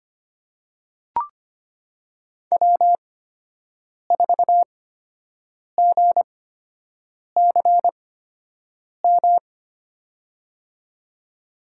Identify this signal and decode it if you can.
Morse code, heard as a keyed tone, about 25 wpm, spaced as for 7 wpm, 700 Hz. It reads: W4GCM
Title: USA Call Signs - 8x Character Spacing - 25wpm